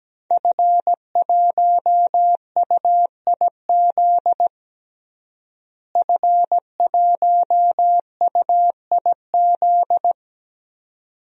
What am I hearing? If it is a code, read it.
Morse code, heard as a keyed tone, about 17 wpm, 700 Hz. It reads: F1UIZ F1UIZ
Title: DXCC-100 Call Signs - 5 Characters - Repeated 2x - 17wpm